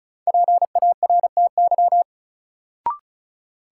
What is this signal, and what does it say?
Morse code, heard as a keyed tone, about 36 wpm, 700 Hz. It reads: PARTY
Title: Sentences from Top 1400 Words - New Word Drill Alt - 35wpm